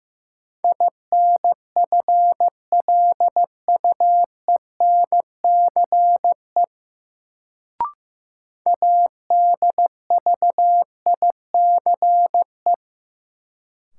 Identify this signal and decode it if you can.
Morse code, heard as a keyed tone, about 15 wpm, 700 Hz. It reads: INFLUENCE ADVICE
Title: Sentences from Top 1100 Words - New Word Drill - 15wpm